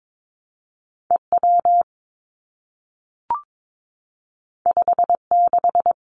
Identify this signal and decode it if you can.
Morse code, heard as a keyed tone, about 22 wpm, 700 Hz. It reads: E W 5 6
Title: Two Challenging Letter-Number Combinations 22wpm